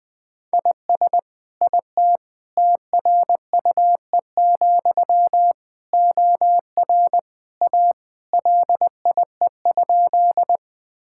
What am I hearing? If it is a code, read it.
Morse code, heard as a keyed tone, about 21 wpm, 700 Hz. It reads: IS IT TRUE, OR A LIE?
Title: Sentences from Top 1600 Words 20wpm